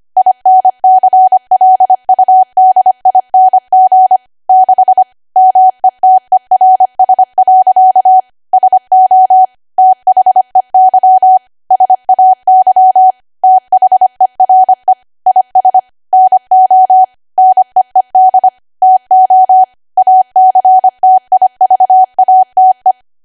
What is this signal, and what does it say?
Morse code, heard as a keyed tone, about 25 wpm, 750 Hz. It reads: INCLUDING 6 METERS. SO THEY SAY THERE IS NO NEED TO ACTIVATE
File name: cwa / W1AW-Level3 / 25WPM